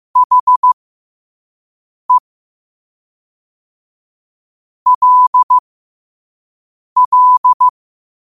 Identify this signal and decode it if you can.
Morse code, heard as a keyed tone, about 15 wpm, spaced as for 5 wpm, 1000 Hz. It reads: HE LL
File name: qotdfiles / 5wpm